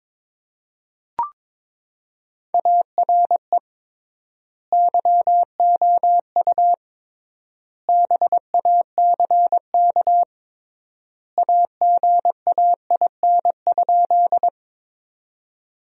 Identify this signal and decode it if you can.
Morse code, heard as a keyed tone, about 22 wpm, 700 Hz. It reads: ARE YOU BACK AGAIN?
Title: 3x Word-Spacing - Sentences from Top 200 Words - 22wpm